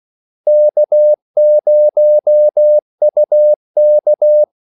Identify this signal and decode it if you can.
Morse code, heard as a keyed tone, about 16 wpm, 600 Hz. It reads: K0UK
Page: Kaldesignaler 16 wpm